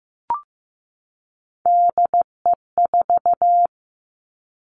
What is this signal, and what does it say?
Morse code, heard as a keyed tone, about 15 wpm, 700 Hz. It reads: DE4
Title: Three Challenging Letter-Number Combinations 15wpm